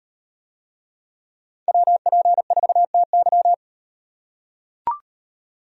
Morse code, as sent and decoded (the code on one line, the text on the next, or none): .-- .--. ....- - -.--
WP4TY